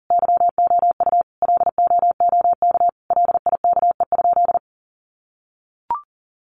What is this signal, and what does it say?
Morse code, heard as a keyed tone, about 40 wpm, 700 Hz. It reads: YOU LOOK LIKE?
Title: Sentences from Top 100 Words 40wpm